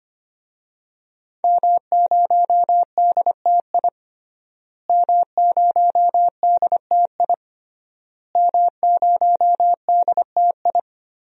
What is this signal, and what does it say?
Morse code, heard as a keyed tone, about 25 wpm, 700 Hz. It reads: M0DTI M0DTI M0DTI
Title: DXCC-100 Call Signs - 5 Characters - Repeated 3x - 25wpm